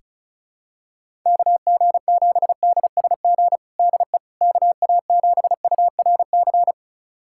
Morse code, heard as a keyed tone, about 36 wpm, 700 Hz. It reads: KG7DSG DE KA7URC
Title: USA Callsign DE Callsign - 6 Characters - 35wpm